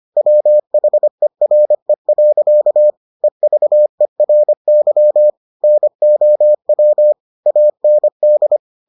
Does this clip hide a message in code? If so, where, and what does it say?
Yes: Morse code from 0.2 to 8.6 s, heard as a keyed tone, about 25 wpm, 600 Hz. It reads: WHERE. EVERY NOW AND